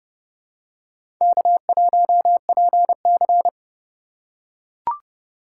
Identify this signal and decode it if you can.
Morse code, heard as a keyed tone, about 29 wpm, 700 Hz. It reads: K1PC
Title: USA Call Signs - 4 Characters - 8x Character Spacing - 30wpm